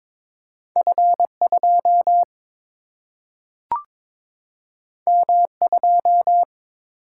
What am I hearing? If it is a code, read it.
Morse code, heard as a keyed tone, about 22 wpm, 700 Hz. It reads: F 2 M 2
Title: Single-Letter Single-Number 22wpm